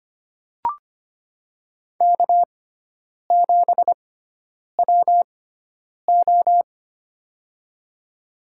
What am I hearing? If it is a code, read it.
Morse code, heard as a keyed tone, about 25 wpm, spaced as for 9 wpm, 700 Hz. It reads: K7WO
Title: USA Call Signs - 6x Character Spacing - 25wpm